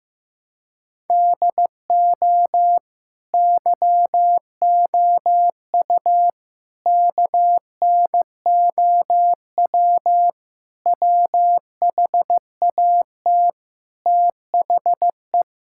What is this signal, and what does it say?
Morse code, heard as a keyed tone, about 15 wpm, 700 Hz. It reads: DO YOU KNOW WHAT THE
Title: Sentences from Top 1600 Words 15wpm